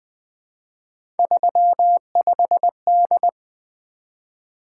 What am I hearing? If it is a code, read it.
Morse code, heard as a keyed tone, about 20 wpm, 700 Hz. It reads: 35D
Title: Three Letter-Number Combinations - 20wpm